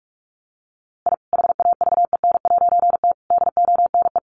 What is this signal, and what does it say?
Morse code, heard as a keyed tone, about 44 wpm, 700 Hz. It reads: I HAVEN'T DONE
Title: A Sound Of Thunder 45wpm